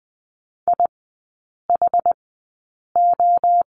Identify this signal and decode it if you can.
Morse code, heard as a keyed tone, about 20 wpm, spaced as for 8 wpm, 700 Hz. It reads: IHO
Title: 093 Lesson 16e - Letter c - Three Characters 20wpm